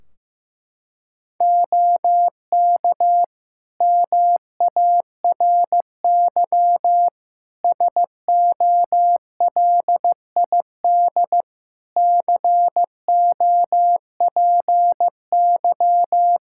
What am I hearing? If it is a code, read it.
Morse code, heard as a keyed tone, about 15 wpm, 700 Hz. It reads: OK MARY SOLID COPY
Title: Practice QSO - Exchange 3 - More Info - 15wpm